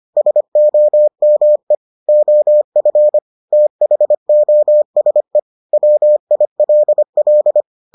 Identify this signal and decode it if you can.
Morse code, heard as a keyed tone, about 25 wpm, 600 Hz. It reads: SOME OF THOSE WILL